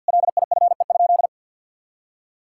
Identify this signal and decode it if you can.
Morse code, heard as a keyed tone, about 46 wpm, 700 Hz. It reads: LIFE?
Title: Sentences from Top 1000 Words - Review Part 5 - 50wpm